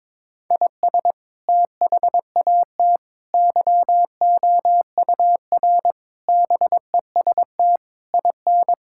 Morse code, heard as a keyed tone, about 22 wpm, 700 Hz. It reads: IS THAT YOUR BEST IN
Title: Sentences from Top 1000 Words - Review Part 5 - 22wpm